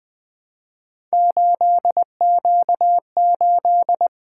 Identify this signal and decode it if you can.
Morse code, heard as a keyed tone, about 20 wpm, 700 Hz. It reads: 8Q8
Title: Three Letter-Number Combinations - 20wpm